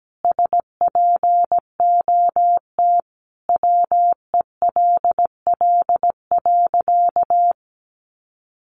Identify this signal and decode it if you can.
Morse code, heard as a keyed tone, about 17 wpm, 700 Hz. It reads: SPOT WELL.
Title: Sentences from Top 1100 Words 17wpm